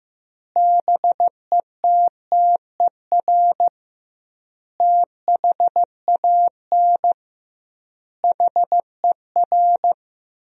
Morse code, heard as a keyed tone, about 15 wpm, 700 Hz. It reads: BETTER THAN HER
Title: 2x Word-Spacing - Sentences from Top 400 Words - 15wpm